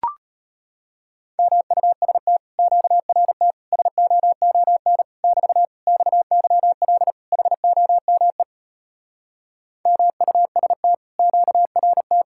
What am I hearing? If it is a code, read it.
Morse code, heard as a keyed tone, about 33 wpm, 700 Hz. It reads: MUST QRT SOON <BT> XYL HOME MUST QRT